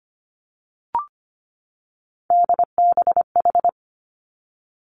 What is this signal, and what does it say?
Morse code, heard as a keyed tone, about 25 wpm, 700 Hz. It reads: DBH
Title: Three Challenging Letter-Number Combinations 25wpm